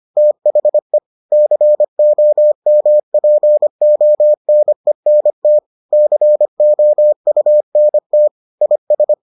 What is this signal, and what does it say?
Morse code, heard as a keyed tone, about 25 wpm, 600 Hz. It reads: THE COMPONENT COUNT IS